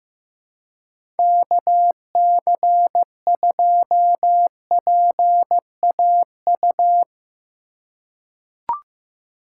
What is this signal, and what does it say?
Morse code, heard as a keyed tone, about 16 wpm, 700 Hz. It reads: KC2PAU
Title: USA Call Signs - 6 Characters - 2x Character Spacing - 15wpm